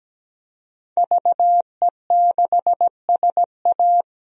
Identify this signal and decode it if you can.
Morse code, heard as a keyed tone, about 17 wpm, 700 Hz. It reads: VE6SA
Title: DXCC-100 Call Signs - 5 Characters - Repeated 3x - 17wpm